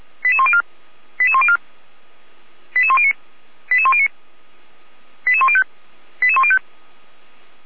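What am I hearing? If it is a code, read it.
Five-tone selective call (ZVEI), heard as quick runs of five tones, tones of about 70 ms. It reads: 88104 88104 88109 88109 88105 88105
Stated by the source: ZVEI five-tone selective call 88405 received with an SDR.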